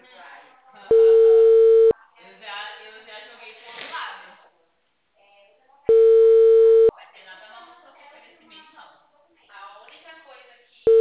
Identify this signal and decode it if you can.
Ringback tone (China, 450 Hz), with speech.